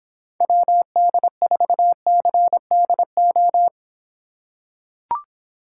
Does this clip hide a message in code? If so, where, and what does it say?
Yes: Morse code from 0.4 to 3.7 s, heard as a keyed tone, about 25 wpm, 700 Hz. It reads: WD4CDO